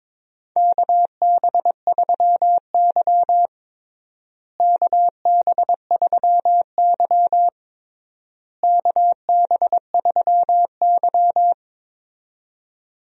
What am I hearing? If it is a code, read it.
Morse code, heard as a keyed tone, about 22 wpm, 700 Hz. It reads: KB3Y KB3Y KB3Y
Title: USA Call Signs - 4 Characters - Repeated 3x - 22wpm